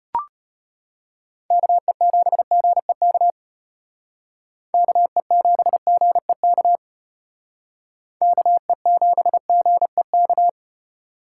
Morse code, heard as a keyed tone, about 31 wpm, 700 Hz. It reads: KE7GEK KE7GEK KE7GEK
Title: Speed Racing - USA Call Signs 25wpm